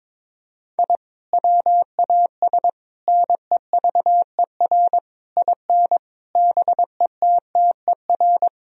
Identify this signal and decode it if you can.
Morse code, heard as a keyed tone, about 21 wpm, 700 Hz. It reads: I WAS NEVER IN BETTER
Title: Sentences from Top 1000 Words - Review Part 3 - 22wpm